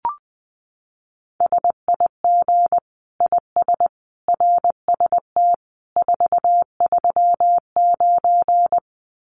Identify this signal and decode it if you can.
Morse code, heard as a keyed tone, about 20 wpm, 700 Hz. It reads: SIG IS RST 439
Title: Common QSO Phrases 20wpm